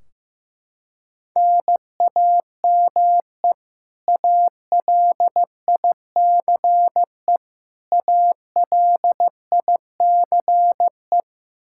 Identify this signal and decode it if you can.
Morse code, heard as a keyed tone, about 15 wpm, 700 Hz. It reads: NAME ALICE ALICE
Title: Practice QSO - Exchange 1 - Beginning - 15wpm